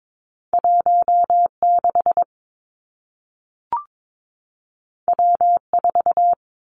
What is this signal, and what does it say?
Morse code, heard as a keyed tone, about 22 wpm, 700 Hz. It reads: 1 6 W 4
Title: Two Challenging Letter-Number Combinations 22wpm